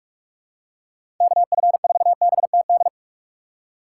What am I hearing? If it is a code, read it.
Morse code, heard as a keyed tone, about 44 wpm, 700 Hz. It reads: KF4BTD
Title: USA Call Signs - 6 Characters - 3x Character Spacing - 45wpm